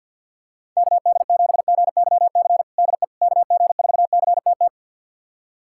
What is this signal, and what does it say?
Morse code, heard as a keyed tone, about 52 wpm, 700 Hz. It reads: KD7GYC DE KG4CTT
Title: USA Callsign DE Callsign - 6 Characters - 50wpm